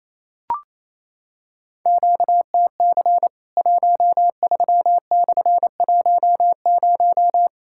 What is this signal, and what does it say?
Morse code, heard as a keyed tone, about 28 wpm, 700 Hz. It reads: QTC 13/10
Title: Worked All Europe DX Contest - QTC Exchange - 28wpm